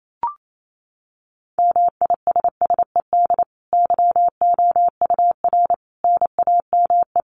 Morse code, heard as a keyed tone, about 28 wpm, 700 Hz. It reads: MISSED YOUR NAME
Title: Common QSO Phrases 28wpm